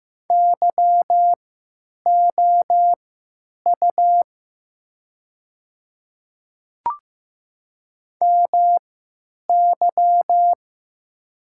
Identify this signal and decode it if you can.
Morse code, heard as a keyed tone, about 15 wpm, spaced as for 9 wpm, 700 Hz. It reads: YOU MY
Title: 3x Character Spacing - Top 300 Words - 15wpm